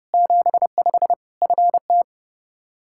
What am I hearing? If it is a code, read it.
Morse code, heard as a keyed tone, about 32 wpm, 700 Hz. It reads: 75 FT
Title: Practice QSO - Exchange 2 - Info - 30wpm